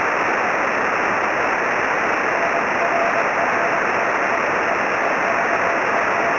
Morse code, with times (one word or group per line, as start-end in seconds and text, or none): none